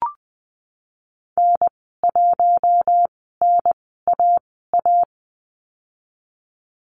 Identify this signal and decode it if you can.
Morse code, heard as a keyed tone, about 20 wpm, spaced as for 14 wpm, 700 Hz. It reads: N1NAA
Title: 088 Lesson 15g - Number 5 - Call Signs 20wpm